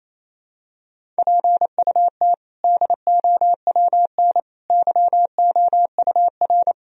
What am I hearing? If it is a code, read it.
Morse code, heard as a keyed tone, about 28 wpm, 700 Hz. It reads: PUT DOWN YOUR
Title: Sentences from Top 1000 Words - Review Part 1 - 28wpm